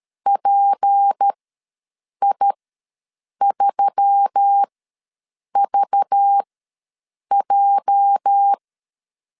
Morse code, heard as a keyed tone, about 13 wpm, spaced as for 7 wpm, 800 Hz. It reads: PI3VJ
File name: LEZIONI / 8 YJ45 / 8 YJ ws 12wpm 3342